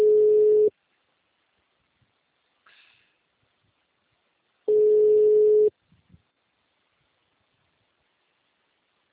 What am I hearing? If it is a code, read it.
Ringback tone (Europe, 425 Hz).